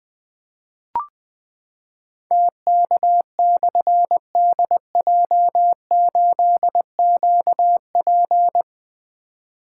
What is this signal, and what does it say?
Morse code, heard as a keyed tone, about 20 wpm, 700 Hz. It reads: TK/DJ8QP
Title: Difficult Call Signs 20wpm